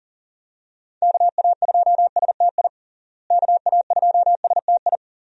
Morse code, heard as a keyed tone, about 37 wpm, 700 Hz. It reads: KA2STI KA2STI
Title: USA Call Signs - 6 Characters - Repeated 2x - 40wpm